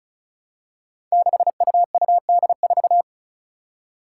Morse code, heard as a keyed tone, about 36 wpm, 700 Hz. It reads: 6UUD4
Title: Five Challenging Letter-Number Combinations 35wpm